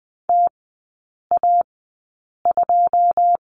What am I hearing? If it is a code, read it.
Morse code, heard as a keyed tone, about 20 wpm, spaced as for 8 wpm, 700 Hz. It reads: TA2
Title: 087 Lesson 15e - Number 5 - Three Characters 20wpm